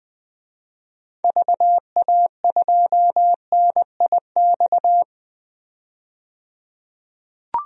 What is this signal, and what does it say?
Morse code, heard as a keyed tone, about 20 wpm, 700 Hz. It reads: VA2NIX
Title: DXCC-100 Call Signs - 6 Characters - 20wpm